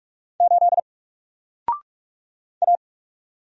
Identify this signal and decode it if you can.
Morse code, heard as a keyed tone, about 44 wpm, 700 Hz. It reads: 8 A
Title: Single Letter-Number 45wpm